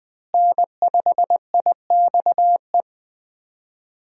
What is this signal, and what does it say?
Morse code, heard as a keyed tone, about 20 wpm, 700 Hz. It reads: N5IXE
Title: USA Call Signs - 5 Characters - Repeated 3x - 20wpm